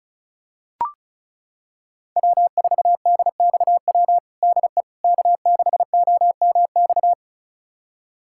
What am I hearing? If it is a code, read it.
Morse code, heard as a keyed tone, about 36 wpm, 700 Hz. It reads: W4DXW DE K6OMX
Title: USA Callsign DE Callsign - 5 Characters - 35wpm